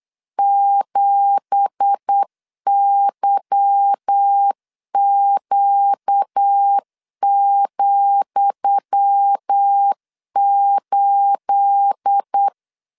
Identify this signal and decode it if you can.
Morse code, heard as a keyed tone, about 8 wpm, 800 Hz. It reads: 7YQ,8
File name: LEZIONI / 10 SIGNS / 10D COMA 8wpm 3749